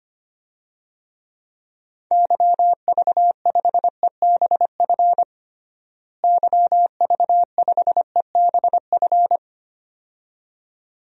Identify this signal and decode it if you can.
Morse code, heard as a keyed tone, about 25 wpm, 700 Hz. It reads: YV5EBF YV5EBF
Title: DXCC-100 Call Signs - 6 Characters - Repeated 2x - 25wpm